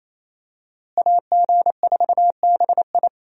Morse code, heard as a keyed tone, about 27 wpm, 700 Hz. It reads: AG4BI
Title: USA Call Signs - 5 Characters - Repeated 3x - 28wpm